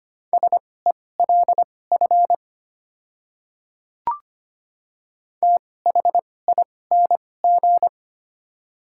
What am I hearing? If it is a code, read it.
Morse code, heard as a keyed tone, about 25 wpm, spaced as for 18 wpm, 700 Hz. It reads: SELF THING
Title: 2x Character Spacing - Top 200 Words - 25wpm